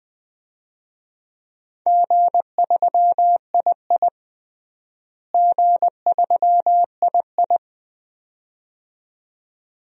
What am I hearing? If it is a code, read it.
Morse code, heard as a keyed tone, about 20 wpm, 700 Hz. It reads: G3II G3II
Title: DXCC-100 Call Signs - 4 Characters - Repeated 2x - 20wpm